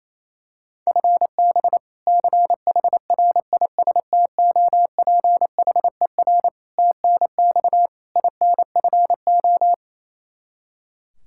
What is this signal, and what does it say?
Morse code, heard as a keyed tone, about 28 wpm, 700 Hz. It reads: FB CHRISTOPHER TNX INFO
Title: K1USN SST QSO Practice 28wpm